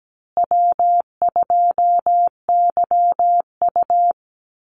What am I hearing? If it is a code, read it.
Morse code, heard as a keyed tone, about 17 wpm, 700 Hz. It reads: W2YU
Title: USA Call Signs Repeated 3x 17wpm